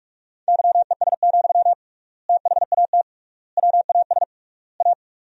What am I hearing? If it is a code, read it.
Morse code, heard as a keyed tone, about 42 wpm, 700 Hz. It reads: YES, THAT WAS A
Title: 3x Word-Spacing - Sentences from Top 500 Words - 45wpm